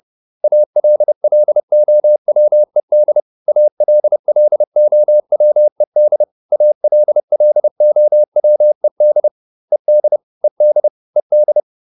Morse code, heard as a keyed tone, about 30 wpm, 600 Hz. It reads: ALLOWED ALLOWED ALLOWED ED ED ED